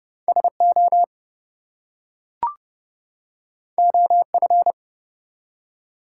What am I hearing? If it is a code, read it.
Morse code, heard as a keyed tone, about 30 wpm, 700 Hz. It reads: SO OF
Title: Two Letter Words - 30wpm